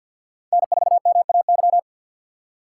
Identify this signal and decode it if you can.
Morse code, heard as a keyed tone, about 61 wpm, 700 Hz. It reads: N4MAY
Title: Speed Racing - USA 1x3 Call Signs 50wpm